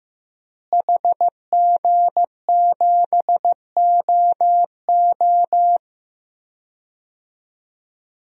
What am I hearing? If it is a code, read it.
Morse code, heard as a keyed tone, about 15 wpm, 700 Hz. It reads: HG7OO
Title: DXCC-100 Call Signs - 5 Characters - Repeated 3x - 15wpm